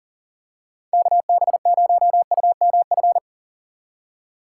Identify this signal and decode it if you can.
Morse code, heard as a keyed tone, about 38 wpm, 700 Hz. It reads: KB0UMF
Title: USA Call Signs - 6 Characters - 4x Character Spacing - 40wpm